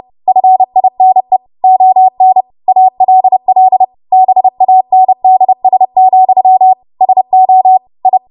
Morse code, heard as a keyed tone, about 30 wpm, 750 Hz. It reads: FINE ON ALL BANDS, SO I
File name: Archive / 30 WPM / 30WPM